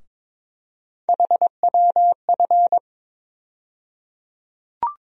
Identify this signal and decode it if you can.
Morse code, heard as a keyed tone, about 22 wpm, 700 Hz. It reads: HWF
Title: USA Call Sign Postfixes - 3 Letters 22wpm